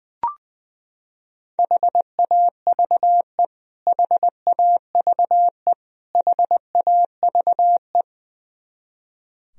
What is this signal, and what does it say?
Morse code, heard as a keyed tone, about 20 wpm, 700 Hz. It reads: HAVE HAVE HAVE
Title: Top 100 Words Repeated 20wpm